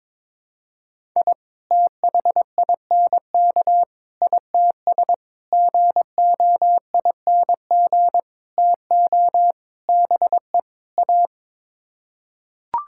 Morse code, heard as a keyed tone, about 22 wpm, 700 Hz. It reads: I THINK ITS GOING TO BE A
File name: practice / 22wpm / Sets of 7 Words - 22wpm